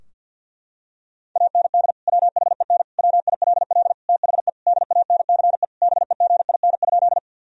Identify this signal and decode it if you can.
Morse code, heard as a keyed tone, about 51 wpm, 700 Hz. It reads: AND WHEN WILL THE DANCE BEGIN?
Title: Sentences from Top 1000 Words - Review Part 4 - 50wpm